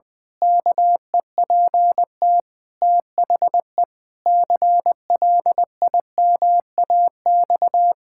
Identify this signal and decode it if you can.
Morse code, heard as a keyed tone, about 20 wpm, 700 Hz. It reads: KEPT THE CLIMAX
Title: Pangram Challenge v2 - 20wpm